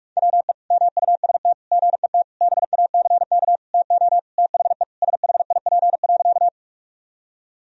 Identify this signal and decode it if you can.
Morse code, heard as a keyed tone, about 41 wpm, 700 Hz. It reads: WE MUST GET BACK TO THE SHIP.
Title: Sentences from Top 1000 Words - Review Part 2 - 45wpm